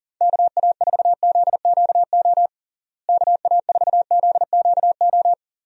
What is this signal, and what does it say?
Morse code, heard as a keyed tone, about 40 wpm, 700 Hz. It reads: KA4ZQO KA4ZQO
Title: USA Call Signs - 6 Characters - Repeated 3x - 40wpm